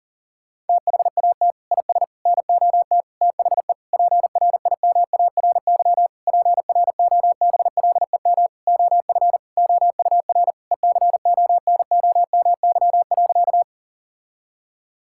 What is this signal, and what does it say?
Morse code, heard as a keyed tone, about 40 wpm, 700 Hz. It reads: THAT IS NOT THE PRIMARY PROBLEM OF OUR ECONOMY.
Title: Sentences from Top 1000 Words - Review Part 5 - 40wpm